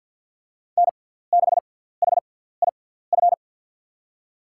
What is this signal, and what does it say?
Morse code, heard as a keyed tone, about 52 wpm, spaced as for 17 wpm, 700 Hz. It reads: N6HIF
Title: USA Call Signs - 5 Characters - 6x Character Spacing - 50wpm